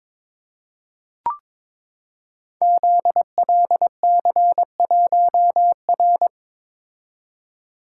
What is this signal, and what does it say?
Morse code, heard as a keyed tone, about 22 wpm, 700 Hz. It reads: ZLC1R
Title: Code Groups - 5 Letter:Numbers - 22wpm